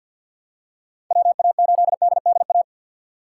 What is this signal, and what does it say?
Morse code, heard as a keyed tone, about 48 wpm, 700 Hz. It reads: WA8DDA
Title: USA Call Signs - 6 Characters - 3x Character Spacing - 50wpm